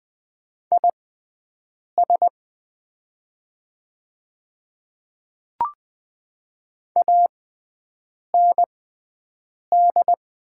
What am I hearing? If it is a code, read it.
Morse code, heard as a keyed tone, about 20 wpm, spaced as for 7 wpm, 690 Hz. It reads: IS AND
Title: 6x Character Spacing - Top 100 Words - 20wpm